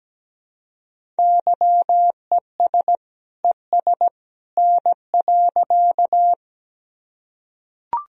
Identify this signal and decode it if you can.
Morse code, heard as a keyed tone, about 17 wpm, 700 Hz. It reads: YES ES N.
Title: Sentences from Top 500 Words and Abbreviations 17wpm